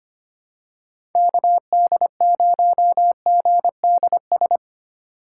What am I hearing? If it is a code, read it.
Morse code, heard as a keyed tone, about 25 wpm, 700 Hz. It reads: KD0GDS
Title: USA Call Signs - 6 Characters - 3x Character Spacing - 25wpm